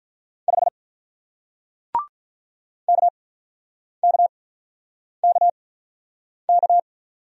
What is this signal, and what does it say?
Morse code, heard as a keyed tone, about 46 wpm, spaced as for 9 wpm, 700 Hz. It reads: 5 KKKK